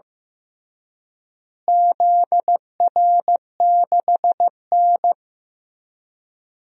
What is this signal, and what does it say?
Morse code, heard as a keyed tone, about 15 wpm, 700 Hz. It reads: ZR6N
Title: DXCC-100 Call Signs - 4 Characters - Repeated 3x - 15wpm